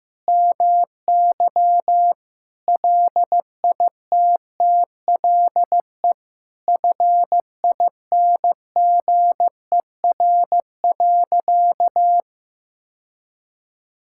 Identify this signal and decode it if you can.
Morse code, heard as a keyed tone, about 15 wpm, 700 Hz. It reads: MY LITTLE FINGER.
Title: Sentences from Top 1600 Words 15wpm